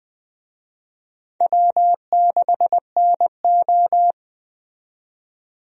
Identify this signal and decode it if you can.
Morse code, heard as a keyed tone, about 21 wpm, 700 Hz. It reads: W6NO
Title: USA Call Signs - 4 Characters - 20wpm